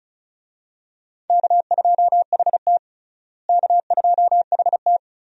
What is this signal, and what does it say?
Morse code, heard as a keyed tone, about 34 wpm, 700 Hz. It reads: K2HT K2HT
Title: USA Call Signs - 4 Characters - Repeated 3x - 35wpm